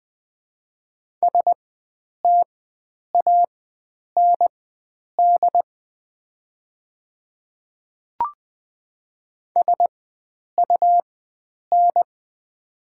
Morse code, heard as a keyed tone, about 20 wpm, spaced as for 9 wpm, 700 Hz. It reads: STAND SUN
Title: 4x Character Spacing - Top 300 Words - 20wpm